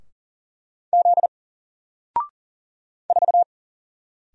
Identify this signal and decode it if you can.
Morse code, heard as a keyed tone, about 40 wpm, 700 Hz. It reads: Z 4
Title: Single Character (Letter:Number:Punctuation:Prosign) 40wpm